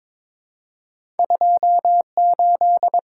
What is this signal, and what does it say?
Morse code, heard as a keyed tone, about 22 wpm, 700 Hz. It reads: 28
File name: practice / 22wpm / Two Letter-Number Combinations - 22wpm